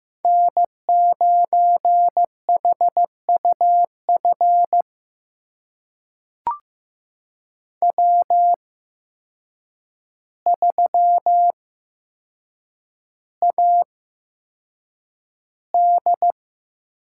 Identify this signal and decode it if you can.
Morse code, heard as a keyed tone, about 15 wpm, spaced as for 7 wpm, 700 Hz. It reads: N9HUF W3AD
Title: USA Call Signs - 8x Character Spacing - 15wpm